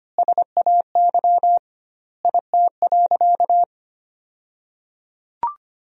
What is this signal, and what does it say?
Morse code, heard as a keyed tone, about 25 wpm, 700 Hz. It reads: SAY IT.
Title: 2x Word-Spacing - Sentences from Top 400 Words - 25wpm